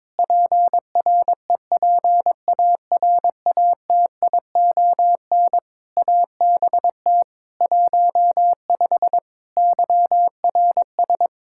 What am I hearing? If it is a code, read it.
Morse code, heard as a keyed tone, about 22 wpm, 700 Hz. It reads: PREPARATION ABT 15 YRS
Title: K1USN SST QSO Practice 22wpm